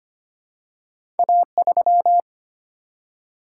Morse code, heard as a keyed tone, about 25 wpm, 700 Hz. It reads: A3
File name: Two Letter-Number Combinations - 25wpm